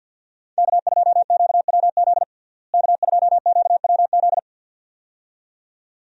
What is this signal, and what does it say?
Morse code, heard as a keyed tone, about 47 wpm, 700 Hz. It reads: K2QWZ K2QWZ
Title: USA Call Signs - 5 Characters - Repeated 2x - 50wpm